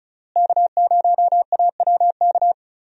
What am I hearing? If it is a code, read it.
Morse code, heard as a keyed tone, about 36 wpm, 700 Hz. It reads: K0AWK
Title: USA Call Signs - 5 Characters - Repeated 3x - 35wpm